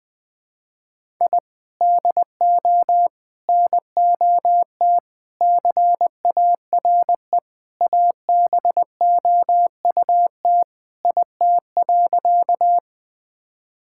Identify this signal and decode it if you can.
Morse code, heard as a keyed tone, about 20 wpm, 700 Hz. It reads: I DO NOT CARE ABOUT IT.
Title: Sentences with 4-Word Phrases - 20wpm